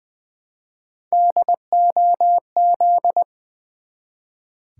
Morse code, heard as a keyed tone, about 20 wpm, 700 Hz. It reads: DOZ
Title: Build A Word - 1 Letter At A Time - 5-Letter Words - 20wpm